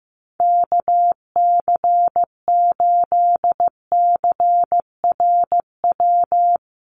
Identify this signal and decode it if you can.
Morse code, heard as a keyed tone, about 15 wpm, 700 Hz. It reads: KC8CRW
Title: USA Call Signs Repeated 2x 15wpm